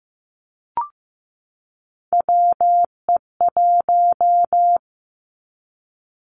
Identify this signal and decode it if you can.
Morse code, heard as a keyed tone, about 15 wpm, 700 Hz. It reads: WE1
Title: Three Challenging Letter-Number Combinations 15wpm